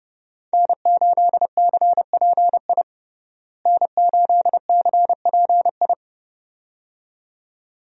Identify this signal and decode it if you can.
Morse code, heard as a keyed tone, about 30 wpm, 700 Hz. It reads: N8CPI N8CPI
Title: USA Call Signs - 5 Characters - Repeated 2x - 30wpm